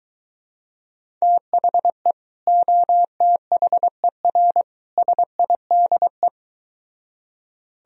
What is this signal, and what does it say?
Morse code, heard as a keyed tone, about 24 wpm, 700 Hz. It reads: THE OTHER SIDE